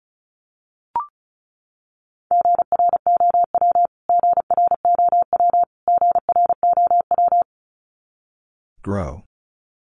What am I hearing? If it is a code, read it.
Morse code, heard as a keyed tone, about 34 wpm, 700 Hz. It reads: GROW GROW GROW
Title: Top 500 Words Repeated 35wpm